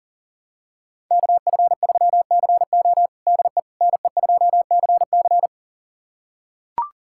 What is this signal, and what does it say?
Morse code, heard as a keyed tone, about 40 wpm, 700 Hz. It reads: KF3CO DE NE2CC
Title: USA Callsign DE Callsign 40wpm